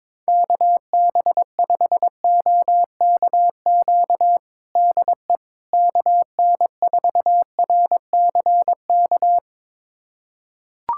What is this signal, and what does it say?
Morse code, heard as a keyed tone, about 22 wpm, 700 Hz. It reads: KB5OKQ DE KN4RCK
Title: USA Callsign DE Callsign 22wpm